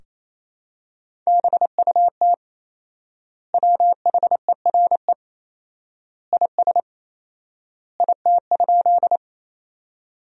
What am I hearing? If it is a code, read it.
Morse code, heard as a keyed tone, about 28 wpm, 700 Hz. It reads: BUT WHERE IS IT?
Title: 4x Word-Spacing - Sentences from Top 200 Words - 28wpm